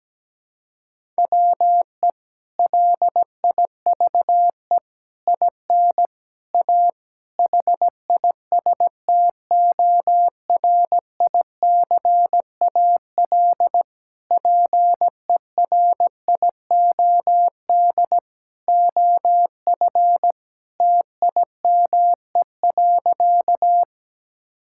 Morse code, heard as a keyed tone, about 17 wpm, 700 Hz. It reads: WE LIVE IN A HISTORICAL PERIOD OF TIME.
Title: Sentences from Top 1000 Words - Review Part 5 - 17wpm